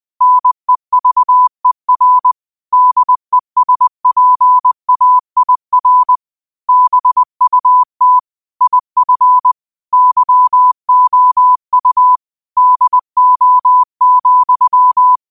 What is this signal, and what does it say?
Morse code, heard as a keyed tone, about 20 wpm, 1000 Hz. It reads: NEVER DESPAIR BUT IF YOU DO,
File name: qotdfiles / 20wpm